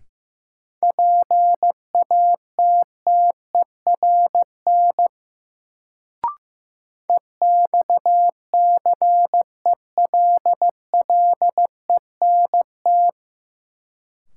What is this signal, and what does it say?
Morse code, heard as a keyed tone, about 15 wpm, 700 Hz. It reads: PATTERN EXCELLENT
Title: Sentences from Top 1000 Words - New Word Drill - 15wpm